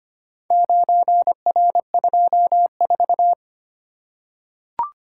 Morse code, heard as a keyed tone, about 25 wpm, 700 Hz. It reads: 9R24
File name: practice / 25wpm / Four Letter-Number Combinations - 25wpm